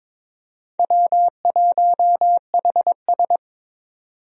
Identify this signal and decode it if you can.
Morse code, heard as a keyed tone, about 22 wpm, 700 Hz. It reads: W1HS
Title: USA Call Signs - 4 Characters - 22wpm